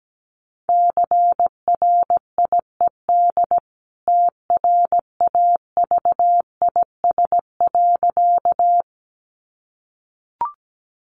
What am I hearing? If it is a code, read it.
Morse code, heard as a keyed tone, about 17 wpm, 700 Hz. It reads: CRIED TRAVIS.
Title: A Sound Of Thunder 17wpm